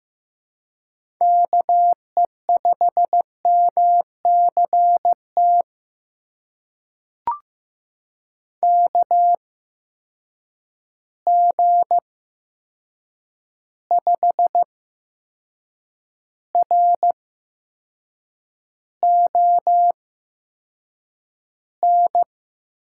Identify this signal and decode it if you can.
Morse code, heard as a keyed tone, about 15 wpm, spaced as for 6 wpm, 700 Hz. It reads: KE5MCT KG5RON
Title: USA Call Signs - 6 Characters - 8x Character Spacing - 15wpm